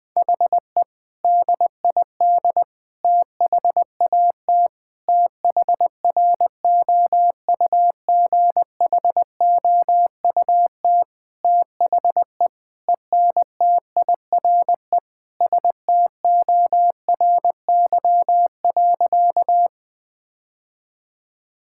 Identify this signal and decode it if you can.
Morse code, heard as a keyed tone, about 21 wpm, 700 Hz. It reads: HE DID THAT THROUGHOUT THE ENTIRE STORY.
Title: Sentences from Top 1000 Words - Review Part 5 - 20wpm